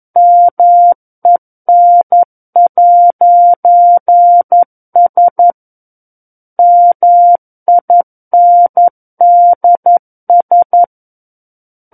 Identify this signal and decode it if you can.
Morse code, heard as a keyed tone, about 11 wpm, 700 Hz. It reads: MEN'S MINDS
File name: Book One / 02-Chapter 2 - 11 WPM